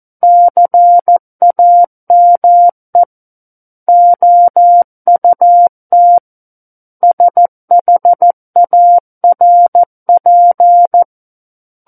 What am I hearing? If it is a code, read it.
Morse code, heard as a keyed tone, about 14 wpm, 700 Hz. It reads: CAME OUT SHARP
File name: Book One / 05-Chapter 5 - 14 WPM